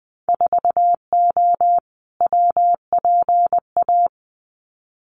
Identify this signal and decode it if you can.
Morse code, heard as a keyed tone, about 20 wpm, 700 Hz. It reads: 4O WPA
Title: Winter Field Day Exchanges 20wpm